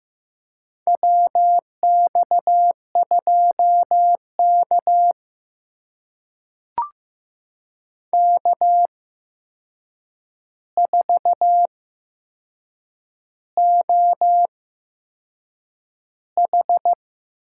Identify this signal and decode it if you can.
Morse code, heard as a keyed tone, about 15 wpm, spaced as for 6 wpm, 700 Hz. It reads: WX2K K4OH
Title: USA Call Signs - 4 Characters - 8x Character Spacing - 15wpm